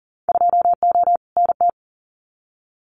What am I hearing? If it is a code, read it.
Morse code, heard as a keyed tone, about 41 wpm, 700 Hz. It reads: 2O NT
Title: Winter Field Day Exchanges Encore 40wpm